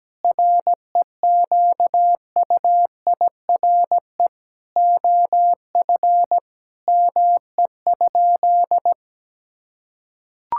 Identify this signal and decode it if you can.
Morse code, heard as a keyed tone, about 17 wpm, 700 Hz. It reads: REQUIRE OF ME?
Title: Sentences from Top 1000 Words - Review Part 6 - 17wpm